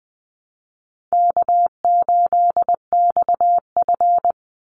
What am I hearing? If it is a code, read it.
Morse code, heard as a keyed tone, about 20 wpm, 700 Hz. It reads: K8XF
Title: Call Signs Repeated 3x - 20wpm